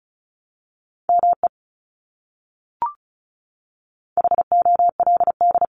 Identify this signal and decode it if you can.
Morse code, heard as a keyed tone, about 34 wpm, 700 Hz. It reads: ME HOLD
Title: Top 500 Words Repeated 35wpm